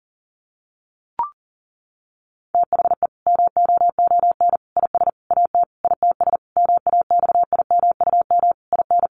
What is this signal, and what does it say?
Morse code, heard as a keyed tone, about 40 wpm, 700 Hz. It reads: THE MOON IS AT ITS MAXIMUM IN
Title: Sentences from Top 1300 Words 40wpm